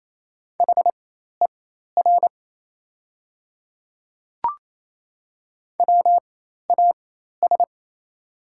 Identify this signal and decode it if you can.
Morse code, heard as a keyed tone, about 28 wpm, spaced as for 13 wpm, 700 Hz. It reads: HER WAS
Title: 4x Character Spacing - Top 100 Words - 28wpm